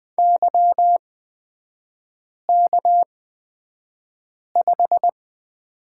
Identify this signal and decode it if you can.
Morse code, heard as a keyed tone, about 20 wpm, spaced as for 5 wpm, 700 Hz. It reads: YK5
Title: 250 Lesson 40e - BK - Three Characters 20wpm